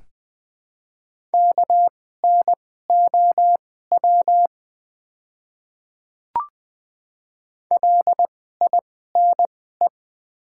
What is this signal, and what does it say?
Morse code, heard as a keyed tone, about 20 wpm, spaced as for 14 wpm, 700 Hz. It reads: KNOW LINE
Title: 2x Character Spacing - Top 200 Words - 20wpm